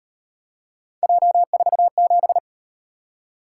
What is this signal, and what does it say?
Morse code, heard as a keyed tone, about 38 wpm, 700 Hz. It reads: J47